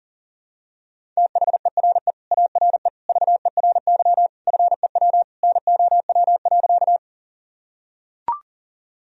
Morse code, heard as a keyed tone, about 37 wpm, 700 Hz. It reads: THERE ARE VERY FEW NOW.
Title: Sentences from Top 1000 Words - Review Part 1 - 40wpm